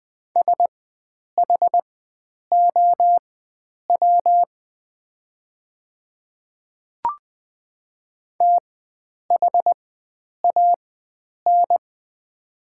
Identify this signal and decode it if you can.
Morse code, heard as a keyed tone, about 20 wpm, spaced as for 9 wpm, 700 Hz. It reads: SHOW THAN
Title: 4x Character Spacing - Top 300 Words - 20wpm